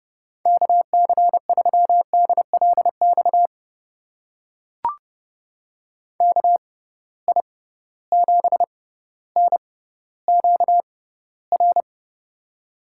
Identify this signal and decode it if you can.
Morse code, heard as a keyed tone, about 30 wpm, spaced as for 15 wpm, 700 Hz. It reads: KC3DLX KI7NQR
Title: USA Call Signs - 6 Characters - 6x Character Spacing - 30wpm